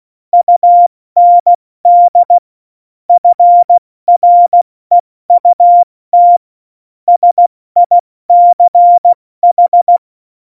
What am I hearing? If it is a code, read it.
Morse code, heard as a keyed tone, about 16 wpm, 700 Hz. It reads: UND FREUT SICH
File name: Morsekurs / Witze / Witz 17. November 2020, 70 F80